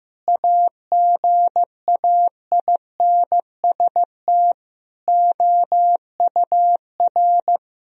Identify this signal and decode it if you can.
Morse code, heard as a keyed tone, about 15 wpm, 700 Hz. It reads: AGAINST OUR
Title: Sentences from Top 1000 Words - Review Part 4 - 15wpm